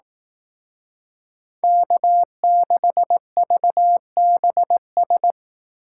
Morse code, heard as a keyed tone, about 18 wpm, 700 Hz. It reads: K6VBS
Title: Speed Racing - USA 1x3 Call Signs 15wpm